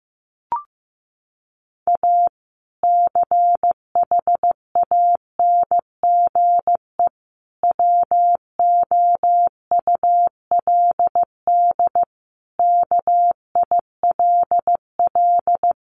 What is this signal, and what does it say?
Morse code, heard as a keyed tone, about 15 wpm, 700 Hz. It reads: A CHANGE WOULD KILL
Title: Sentences from Top 1400 Words 15wpm